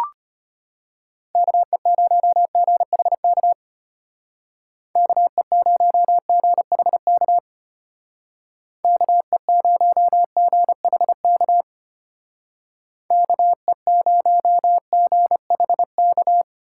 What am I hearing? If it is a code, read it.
Morse code, heard as a keyed tone, about 32 wpm, 700 Hz. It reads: KE0GHK KE0GHK KE0GHK KE0GHK